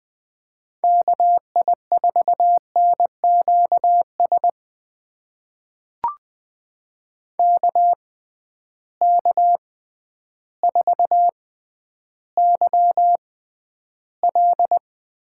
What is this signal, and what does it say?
Morse code, heard as a keyed tone, about 20 wpm, spaced as for 11 wpm, 700 Hz. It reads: KI4NQS KK4YL
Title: USA Call Signs - 6x Character Spacing - 20wpm